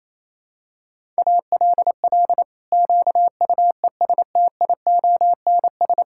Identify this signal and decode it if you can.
Morse code, heard as a keyed tone, about 28 wpm, 700 Hz. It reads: ALL QUESTIONS
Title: Pangram Challenge v2 - 28wpm